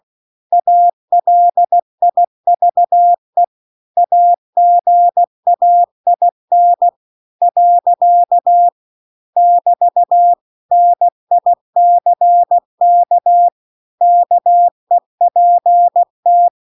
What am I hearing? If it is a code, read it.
Morse code, heard as a keyed tone, about 16 wpm, 700 Hz. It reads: ALIVE AGAIN . <BT> NICK KEPT